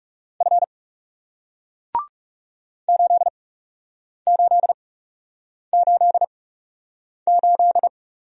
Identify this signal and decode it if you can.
Morse code, heard as a keyed tone, about 41 wpm, spaced as for 8 wpm, 700 Hz. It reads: F 8888